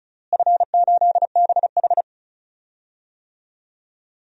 Morse code, heard as a keyed tone, about 36 wpm, 700 Hz. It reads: F8BH
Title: DXCC-100 Call Signs - 4 Characters - Repeated 3x - 35wpm